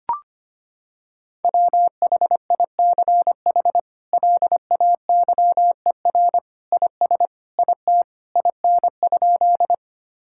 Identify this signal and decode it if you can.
Morse code, heard as a keyed tone, about 25 wpm, 700 Hz. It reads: WHICH LAYER IS IT IN?
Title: Sentences from Top 1600 Words 25wpm